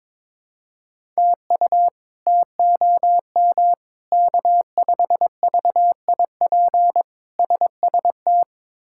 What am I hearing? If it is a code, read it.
Morse code, heard as a keyed tone, about 22 wpm, 700 Hz. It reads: TU TOM K5VIP SST
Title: K1USN SST QSO Practice - 22wpm